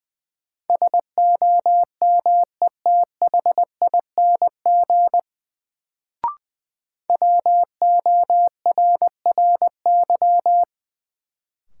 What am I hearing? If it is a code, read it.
Morse code, heard as a keyed tone, about 20 wpm, 700 Hz. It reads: SOMETHING WORRY
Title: Sets of 5 Words - New Word Drill - 20wpm